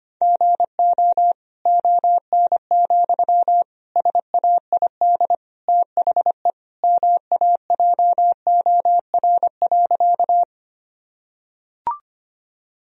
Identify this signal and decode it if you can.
Morse code, heard as a keyed tone, about 25 wpm, 700 Hz. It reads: GO ON, SAID THE MAJOR.
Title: Sentences from Top 1000 Words - Review Part 4 - 25wpm